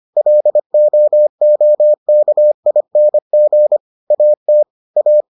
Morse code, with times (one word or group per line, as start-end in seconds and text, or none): 0.2-3.8 LOOKING
4.1-4.6 AT
5.0-5.2 A